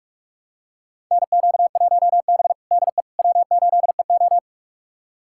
Morse code, heard as a keyed tone, about 41 wpm, 700 Hz. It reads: NQ1B DE W8EO
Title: USA Callsign DE Callsign - 4 Characters - 45wpm